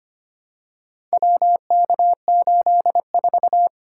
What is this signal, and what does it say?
Morse code, heard as a keyed tone, about 25 wpm, 700 Hz. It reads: WK84
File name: practice / 25wpm / Four Letter-Number Combinations - 25wpm